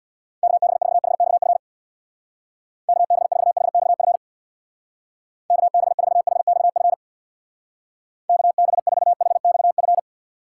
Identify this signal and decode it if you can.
Morse code, heard as a keyed tone, about 63 wpm, 700 Hz. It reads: KB4HXF KB4HXF KB4HXF KB4HXF